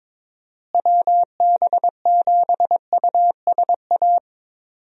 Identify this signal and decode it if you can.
Morse code, heard as a keyed tone, about 22 wpm, 700 Hz. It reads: WB7USA
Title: USA Call Signs - 6 Characters - Repeated 3x - 22wpm